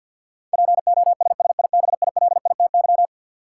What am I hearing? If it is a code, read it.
Morse code, heard as a keyed tone, about 53 wpm, 700 Hz. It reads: POSSIBILITY
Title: Sets of 3 Words - New Word Drill - 50wpm